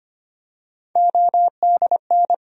Morse code, heard as a keyed tone, about 25 wpm, 700 Hz. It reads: ODN
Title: USA Call Sign Postfixes - 3 Letters 25wpm